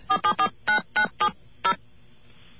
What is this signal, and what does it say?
DTMF, heard as telephone touch tones, tones of about 90 ms. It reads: 4749970